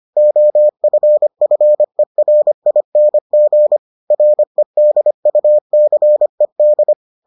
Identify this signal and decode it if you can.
Morse code, heard as a keyed tone, about 25 wpm, 600 Hz. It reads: OFFERING REDUCED